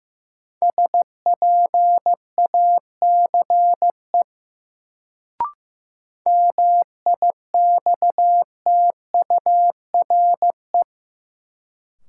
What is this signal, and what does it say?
Morse code, heard as a keyed tone, about 15 wpm, 700 Hz. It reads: SPACE MIXTURE
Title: Sentences from Top 1500 Words - New Word Drill Alt - 15wpm